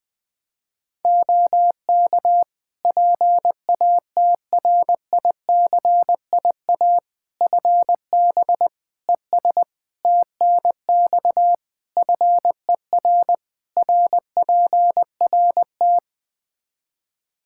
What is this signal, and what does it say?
Morse code, heard as a keyed tone, about 21 wpm, 700 Hz. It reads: OK PATRICIA FB ES TNX FER RPRT
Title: Practice QSO - Exchange 2 - Info - 20wpm